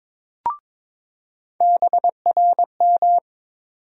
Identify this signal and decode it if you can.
Morse code, heard as a keyed tone, about 22 wpm, 700 Hz. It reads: BRM
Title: USA Call Sign Postfixes - 3 Letters 22wpm